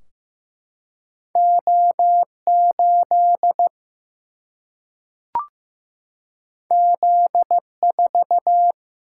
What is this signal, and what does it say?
Morse code, heard as a keyed tone, about 15 wpm, 700 Hz. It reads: O 8 Z 4
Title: Single-Letter Single-Number 15wpm